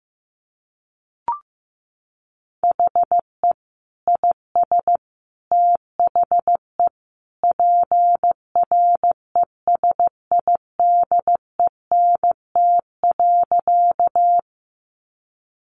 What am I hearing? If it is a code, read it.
Morse code, heard as a keyed tone, about 15 wpm, 700 Hz. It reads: HE IS THE PRESIDENT.
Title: Sentences from Top 1000 Words 15wpm